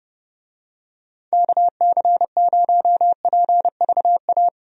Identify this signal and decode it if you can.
Morse code, heard as a keyed tone, about 30 wpm, 700 Hz. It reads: KC0PVA
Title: USA Call Signs - 6 Characters - Repeated 2x - 30wpm